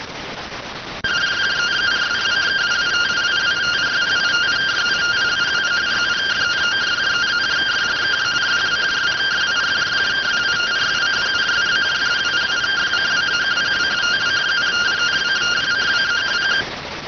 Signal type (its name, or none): two-tone teleprinter signal